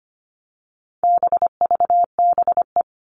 Morse code, heard as a keyed tone, about 25 wpm, 700 Hz. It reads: BVBE
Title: All Cut Numbers - 4-Digit 25wpm